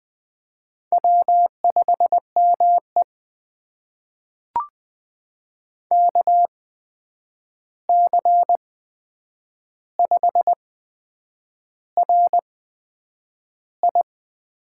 Morse code, heard as a keyed tone, about 20 wpm, spaced as for 8 wpm, 700 Hz. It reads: W5ME KC5RI
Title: USA Call Signs - 8x Character Spacing - 20wpm